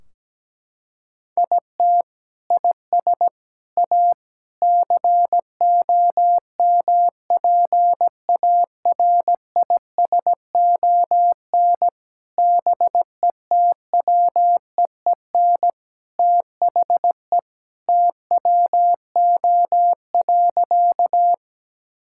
Morse code, heard as a keyed tone, about 17 wpm, 700 Hz. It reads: IT IS A COMPARISON BETWEEN THE TWO.
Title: Sentences from Top 1600 Words 17wpm